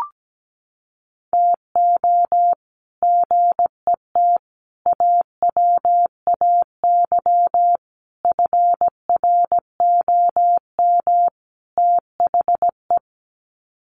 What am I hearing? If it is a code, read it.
Morse code, heard as a keyed tone, about 17 wpm, 700 Hz. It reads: TO GET AWAY FROM THE
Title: Sets of 5 Words Encore2 17wpm